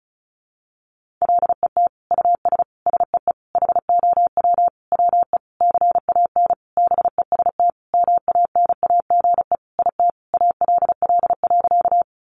Morse code, heard as a keyed tone, about 34 wpm, 700 Hz. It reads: LET US SEE HOW WE CAN BEST MANAGE IT ALL.
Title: Sentences from Top 1400 Words 35wpm